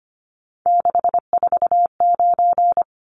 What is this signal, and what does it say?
Morse code, heard as a keyed tone, about 25 wpm, 700 Hz. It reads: 649
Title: Numbers - 3-Digit 25wpm